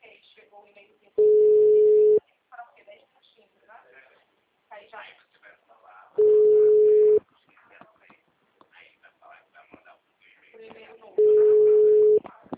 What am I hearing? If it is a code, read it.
Ringback tone (Europe, 425 Hz), with speech.